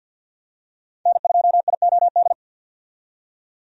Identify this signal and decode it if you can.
Morse code, heard as a keyed tone, about 53 wpm, 700 Hz. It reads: N2IOD
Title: USA Call Signs - 5 Characters - 50wpm